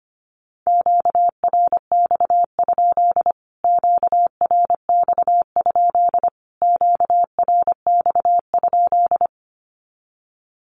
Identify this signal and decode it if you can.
Morse code, heard as a keyed tone, about 24 wpm, 700 Hz. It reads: QRX? QRX? QRX?
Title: Q-Codes Repeated 3x 25wpm